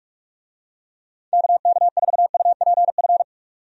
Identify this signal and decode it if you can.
Morse code, heard as a keyed tone, about 42 wpm, 700 Hz. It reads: KK4UPF
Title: USA Call Signs - 6 Characters - 45wpm